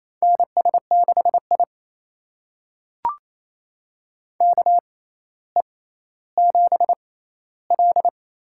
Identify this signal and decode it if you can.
Morse code, heard as a keyed tone, about 28 wpm, spaced as for 14 wpm, 700 Hz. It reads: NS6I KE7L
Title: USA Call Signs - 6x Character Spacing - 28wpm